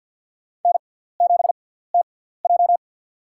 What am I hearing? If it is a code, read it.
Morse code, heard as a keyed tone, about 49 wpm, spaced as for 17 wpm, 700 Hz. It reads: N7TJ
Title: USA Call Signs - 6x Character Spacing - 50wpm